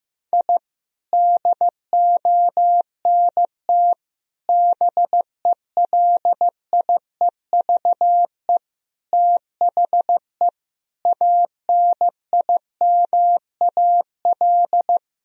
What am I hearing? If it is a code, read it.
Morse code, heard as a keyed tone, about 15 wpm, 700 Hz. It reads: I DONT BELIEVE THE ANIMAL
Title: Sentences from Top 1000 Words - Review Part 1 - 15wpm